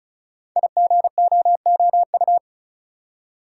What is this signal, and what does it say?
Morse code, heard as a keyed tone, about 36 wpm, 690 Hz. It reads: IGOOU
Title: Code Groups - 5 Letters - 35wpm